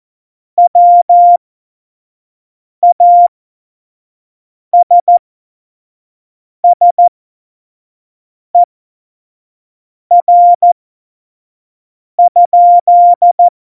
Morse code, heard as a keyed tone, about 14 wpm, spaced as for 5 wpm, 700 Hz. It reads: WASSER?
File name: Morsekurs / Witze / Witz 08. August 17, Tempo 25 F70